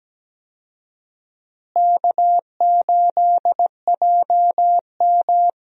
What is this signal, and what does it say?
Morse code, heard as a keyed tone, about 17 wpm, 700 Hz. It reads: K8JM
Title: USA Call Signs - 4 Characters - Repeated 3x - 17wpm